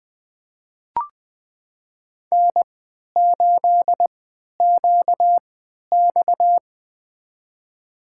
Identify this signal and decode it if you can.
Morse code, heard as a keyed tone, about 20 wpm, spaced as for 11 wpm, 700 Hz. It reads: N8QX
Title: USA Call Signs - 4 Characters - 3x Character Spacing - 20wpm